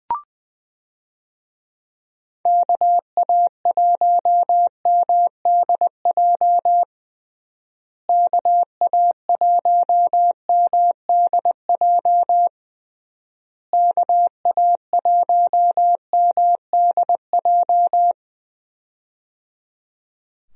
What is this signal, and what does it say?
Morse code, heard as a keyed tone, about 20 wpm, 700 Hz. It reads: KA1MDJ KA1MDJ KA1MDJ
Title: USA Call Signs - 6 Characters - Repeated 3x - 20wpm